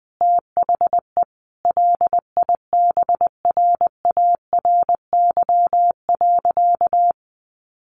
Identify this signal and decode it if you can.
Morse code, heard as a keyed tone, about 19 wpm, 700 Hz. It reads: THE LIBRARY.
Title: Sentences from Top 1400 Words 20wpm